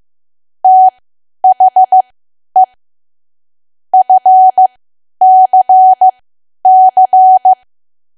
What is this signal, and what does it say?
Morse code, heard as a keyed tone, about 15 wpm, spaced as for 10 wpm, 750 Hz. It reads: THE FCC
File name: cwa / W1AW-Level2 / 10WPM